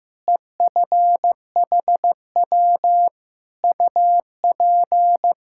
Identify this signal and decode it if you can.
Morse code, heard as a keyed tone, about 16 wpm, 700 Hz. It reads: EFHW UP
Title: K1USN SST QSO Practice 15wpm